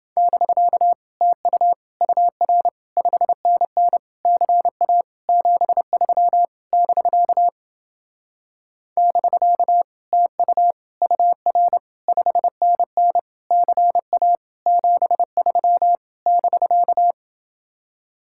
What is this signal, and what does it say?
Morse code, heard as a keyed tone, about 31 wpm, 700 Hz. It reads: <BK> TU UR 5NN CA 73 <BK> <BK> TU UR 5NN CA 73 <BK>